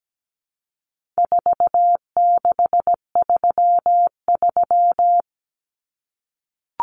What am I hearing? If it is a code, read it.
Morse code, heard as a keyed tone, about 17 wpm, 700 Hz. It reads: 4633
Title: Common Cut Numbers - 4-Digit 17wpm